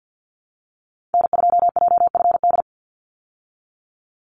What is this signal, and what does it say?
Morse code, heard as a keyed tone, about 48 wpm, 700 Hz. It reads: N2JFD
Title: Call Signs Repeated 3x - 50wpm